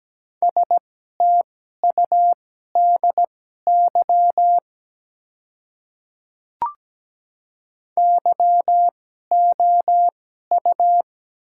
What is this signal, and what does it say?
Morse code, heard as a keyed tone, about 17 wpm, spaced as for 12 wpm, 700 Hz. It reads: STUDY YOU
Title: 2x Character Spacing - Top 300 Words - 17wpm